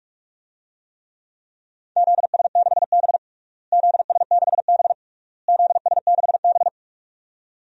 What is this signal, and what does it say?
Morse code, heard as a keyed tone, about 46 wpm, 700 Hz. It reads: ZS6B ZS6B ZS6B
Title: DXCC-100 Call Signs - 4 Characters - Repeated 3x - 45wpm